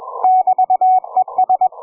Morse code, heard as a keyed tone, about 21 wpm, 740 Hz. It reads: <BT>ES
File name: cw forum / ve1zj 1830 160m 04h00